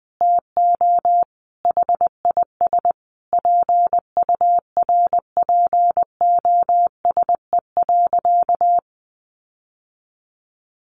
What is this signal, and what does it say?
Morse code, heard as a keyed tone, about 20 wpm, 700 Hz. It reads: TO HIS PURPOSE.
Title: Sentences from Top 1400 Words 20wpm